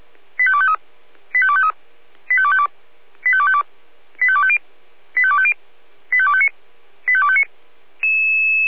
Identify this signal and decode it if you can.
Five-tone selective call (ZVEI), heard as quick runs of five tones, tones of about 70 ms. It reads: 85253 85253 85252 85252 85250 85250 85259 85259